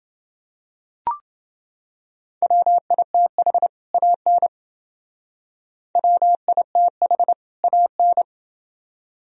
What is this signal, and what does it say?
Morse code, heard as a keyed tone, about 30 wpm, 700 Hz. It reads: WITH AN WITH AN